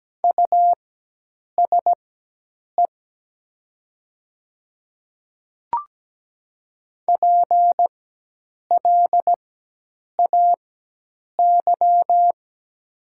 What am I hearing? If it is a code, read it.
Morse code, heard as a keyed tone, about 17 wpm, spaced as for 8 wpm, 700 Hz. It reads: USE PLAY
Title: 4x Character Spacing - Top 300 Words - 17wpm